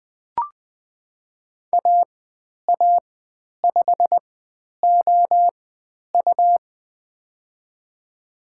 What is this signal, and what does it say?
Morse code, heard as a keyed tone, about 20 wpm, spaced as for 10 wpm, 700 Hz. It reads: AA5OU
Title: 102 Lesson 17g - Letter u - Call Signs 20wpm